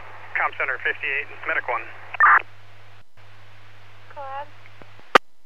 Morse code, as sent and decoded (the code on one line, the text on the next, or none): none